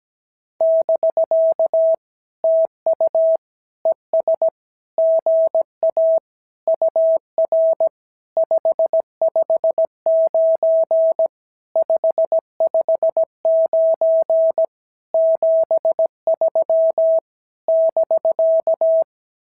Morse code, heard as a keyed tone, about 17 wpm, 650 Hz. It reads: <BK> TU ES GA UR 559 559 73 <BK>
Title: SOTA QSO Practice 17wpm